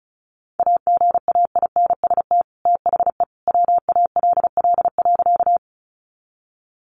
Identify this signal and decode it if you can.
Morse code, heard as a keyed tone, about 34 wpm, 700 Hz. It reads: AGAINST THE WALL.
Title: Sentences from Top 1100 Words 35wpm